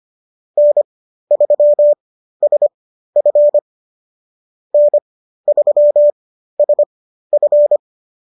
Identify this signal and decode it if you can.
Morse code, heard as a keyed tone, about 25 wpm, spaced as for 13 wpm, 590 Hz. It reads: N3SF N3SF